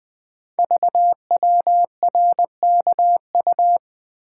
Morse code, heard as a keyed tone, about 20 wpm, 700 Hz. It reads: VWRKU
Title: Code Groups - 5 Letter:Numbers - 20wpm